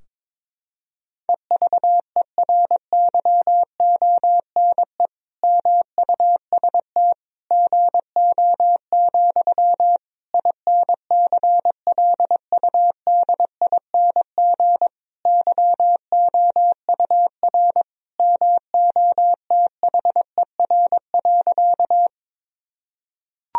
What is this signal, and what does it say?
Morse code, heard as a keyed tone, about 22 wpm, 700 Hz. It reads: EVERYONE MUST GO, INCLUDING YOUR MOTHER.
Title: Sentences from Top 1000 Words - Review Part 4 - 22wpm